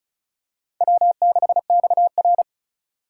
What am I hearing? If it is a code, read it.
Morse code, heard as a keyed tone, about 34 wpm, 700 Hz. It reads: W6XR
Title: K1USN SST QSO Practice - 35wpm